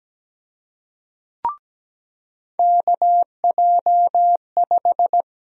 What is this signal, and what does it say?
Morse code, heard as a keyed tone, about 17 wpm, 700 Hz. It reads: KJ5
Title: USA Call Sign Prefixes - 17wpm